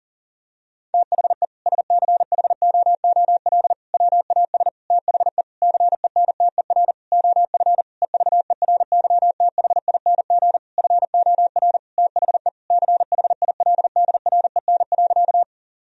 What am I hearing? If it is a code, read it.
Morse code, heard as a keyed tone, about 37 wpm, 700 Hz. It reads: THE SCHOOL WAS THE CENTER OF EVERYTHING FOR THE CHILDREN.
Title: Sentences from Top 1000 Words - Review Part 2 - 40wpm